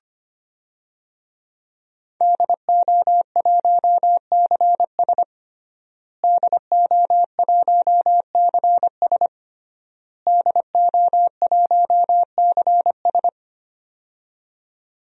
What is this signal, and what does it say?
Morse code, heard as a keyed tone, about 25 wpm, 700 Hz. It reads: DO1CS DO1CS DO1CS
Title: DXCC-100 Call Signs - 5 Characters - Repeated 3x - 25wpm